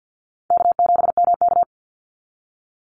Morse code, heard as a keyed tone, about 50 wpm, 700 Hz. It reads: K7MK
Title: NA SOTA Activator Callsigns 50wpm